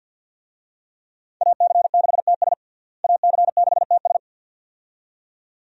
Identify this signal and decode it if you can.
Morse code, heard as a keyed tone, about 47 wpm, 700 Hz. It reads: AC6TS AC6TS
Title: USA Call Signs - 5 Characters - Repeated 2x - 50wpm